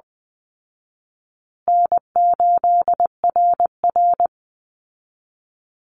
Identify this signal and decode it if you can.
Morse code, heard as a keyed tone, about 20 wpm, 700 Hz. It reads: N8RR
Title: Call Signs Repeated 2x - 20wpm